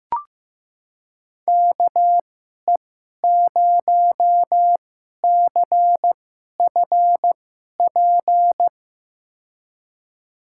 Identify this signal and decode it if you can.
Morse code, heard as a keyed tone, about 15 wpm, spaced as for 11 wpm, 700 Hz. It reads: KE0CFP
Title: USA Call Signs - 6 Characters - 2x Character Spacing - 15wpm